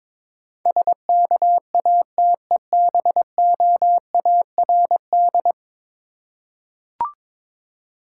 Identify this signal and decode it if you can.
Morse code, heard as a keyed tone, about 22 wpm, 700 Hz. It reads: SKATEBOARD
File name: Difficult Words with Hints - 22wpm